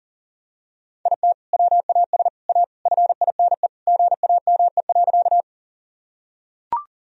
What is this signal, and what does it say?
Morse code, heard as a keyed tone, about 40 wpm, 700 Hz. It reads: IT WAS A FINE GAME.
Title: Sentences from Top 1000 Words - Review Part 3 - 40wpm